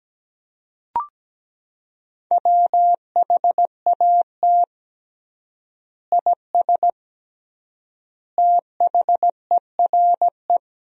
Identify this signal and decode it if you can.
Morse code, heard as a keyed tone, about 17 wpm, 700 Hz. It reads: WHAT IS THERE
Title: 3x Word-Spacing - Sentences from Top 100 Words - 17wpm